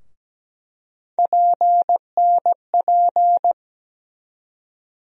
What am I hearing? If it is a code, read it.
Morse code, heard as a keyed tone, about 17 wpm, 700 Hz. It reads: PNP
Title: USA Call Sign Postfixes - 3 Letters 17wpm